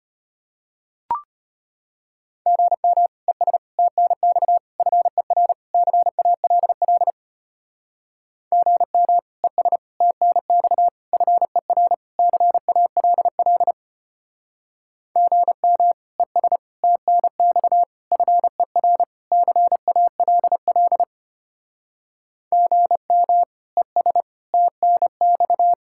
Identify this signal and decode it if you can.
Morse code, heard as a keyed tone, about 32 wpm, 700 Hz. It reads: GM ES TNX FER CALL GM ES TNX FER CALL GM ES TNX FER CALL GM ES TNX